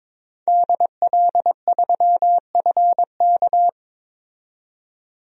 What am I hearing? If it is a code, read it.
Morse code, heard as a keyed tone, about 22 wpm, 700 Hz. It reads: DL3FK
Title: DXCC-100 Call Signs - 5 Characters - 22wpm